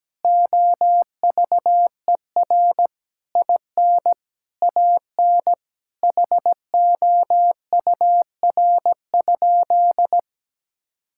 Morse code, heard as a keyed tone, about 17 wpm, 700 Hz. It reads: OVER IN AN HOUR?
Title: Sentences from Top 1000 Words - Review Part 5 - 17wpm